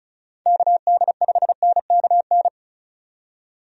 Morse code, heard as a keyed tone, about 36 wpm, 700 Hz. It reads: KD5NKN
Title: USA Call Signs - 6 Characters - 35wpm